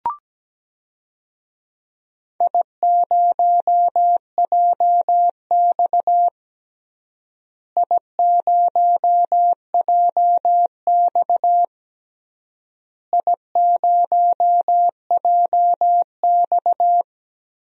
Morse code, heard as a keyed tone, about 17 wpm, 700 Hz. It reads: I0JX I0JX I0JX
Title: DXCC-100 Call Signs - 4 Characters - Repeated 3x - 17wpm